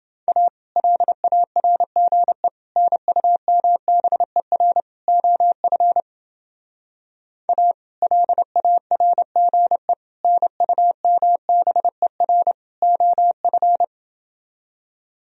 Morse code, heard as a keyed tone, about 30 wpm, 700 Hz. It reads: A LARGE NUMBER OF A LARGE NUMBER OF